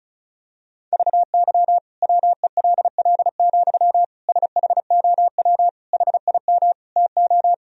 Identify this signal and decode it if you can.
Morse code, heard as a keyed tone, about 36 wpm, 690 Hz. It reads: VY WELL, SHOW HIM TO
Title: Sentences from Top 200 Words and Abbreviations 35wpm